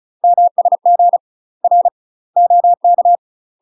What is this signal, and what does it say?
Morse code, heard as a keyed tone, about 36 wpm, 700 Hz. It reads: MSG R OK
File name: UK MASTER QSOs TRAINING / 35WPM / UK MASTER QSOs 0023